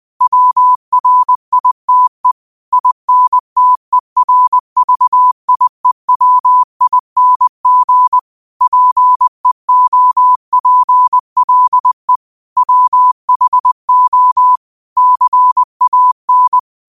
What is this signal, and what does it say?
Morse code, heard as a keyed tone, about 20 wpm, 1000 Hz. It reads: WRITE INTERVIEWING PEOPLE WHO CAN